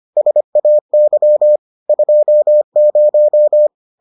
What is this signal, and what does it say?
Morse code, heard as a keyed tone, about 25 wpm, 600 Hz. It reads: SAY 20